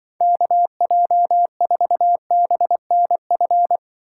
Morse code, heard as a keyed tone, about 24 wpm, 700 Hz. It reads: KJ4BNF
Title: Speed Racing - USA 2x3 Call Signs 20wpm